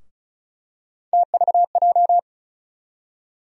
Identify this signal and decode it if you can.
Morse code, heard as a keyed tone, about 34 wpm, 700 Hz. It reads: TVJ
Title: USA Call Sign Postfixes - 3 Letters 35wpm